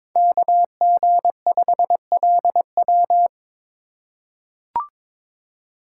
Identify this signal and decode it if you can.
Morse code, heard as a keyed tone, about 21 wpm, 700 Hz. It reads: KG5LW
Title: USA Call Signs - 5 Characters - 2x Character Spacing - 22wpm